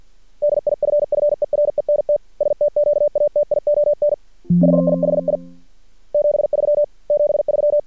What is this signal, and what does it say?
Morse code, heard as a keyed tone, about 50 wpm, 590 Hz. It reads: DIFFERENT STQATION HIHI 73 73
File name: archived / CRSnet